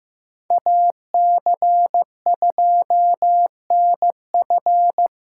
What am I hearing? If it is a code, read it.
Morse code, heard as a keyed tone, about 18 wpm, 700 Hz. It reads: AC2NF
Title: Speed Racing - USA 2x2 Call Signs 15wpm